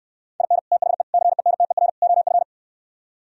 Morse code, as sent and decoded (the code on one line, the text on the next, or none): .. -. / - .... / -.-. -. - . .-. / --- ..-.
IN THE CENTER OF